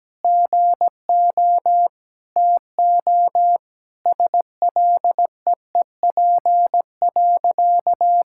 Morse code, heard as a keyed tone, about 17 wpm, 700 Hz. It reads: GO TO SLEEP.
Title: Sentences from Top 1000 Words - Review Part 6 - 17wpm